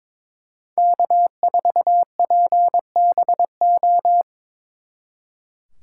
Morse code, heard as a keyed tone, about 26 wpm, 700 Hz. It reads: K4PBO
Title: Speed Racing - USA 1x3 Call Signs 22wpm